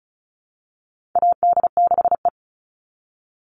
Morse code, heard as a keyed tone, about 34 wpm, 700 Hz. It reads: AD6E
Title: Call Signs Repeated 3x - 35wpm